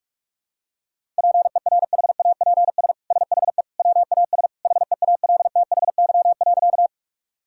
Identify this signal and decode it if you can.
Morse code, heard as a keyed tone, about 46 wpm, 690 Hz. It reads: PERHAPS SHE WAS HEALTHY.
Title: Sentences from Top 1000 Words - Review Part 5 - 45wpm